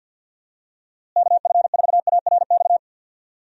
Morse code, heard as a keyed tone, about 52 wpm, 700 Hz. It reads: KF4ARX
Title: USA Call Signs - 6 Characters - Repeated 2x - 50wpm